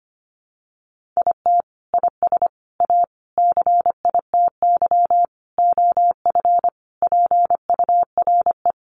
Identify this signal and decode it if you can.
Morse code, heard as a keyed tone, about 25 wpm, 700 Hz. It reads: IT IS A CITY OF PURE
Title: Sentences from Top 1500 Words 25wpm